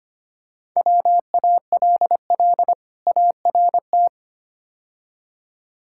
Morse code, heard as a keyed tone, about 30 wpm, 700 Hz. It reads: WALL ART
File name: practice / 25wpm / Difficult Words - Speed-Racing with Hints - 25wpm